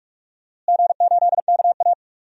Morse code, heard as a keyed tone, about 42 wpm, 700 Hz. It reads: G8KA
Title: DXCC-100 Call Signs - 4 Characters - Repeated 2x - 45wpm